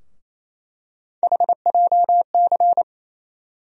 Morse code, heard as a keyed tone, about 28 wpm, 700 Hz. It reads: HJC
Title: USA Call Sign Postfixes - 3 Letters 28wpm